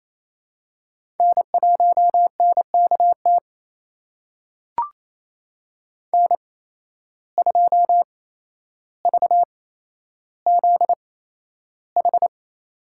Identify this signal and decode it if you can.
Morse code, heard as a keyed tone, about 28 wpm, spaced as for 12 wpm, 700 Hz. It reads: N1NKT N2VZH
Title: USA Call Signs - 5 Characters - 8x Character Spacing - 28wpm